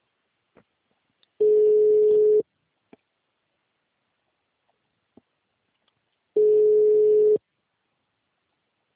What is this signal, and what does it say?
Ringback tone (Europe, 425 Hz).